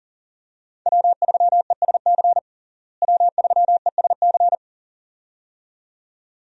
Morse code, heard as a keyed tone, about 40 wpm, 700 Hz. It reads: W3ESC W3ESC
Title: USA Call Signs - 5 Characters - Repeated 2x - 40wpm